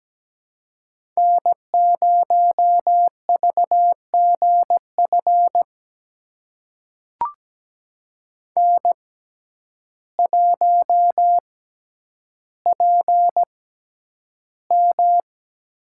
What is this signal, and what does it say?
Morse code, heard as a keyed tone, about 17 wpm, spaced as for 9 wpm, 700 Hz. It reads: N0VGF N1PM
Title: USA Call Signs - 5 Characters - 6x Character Spacing - 17wpm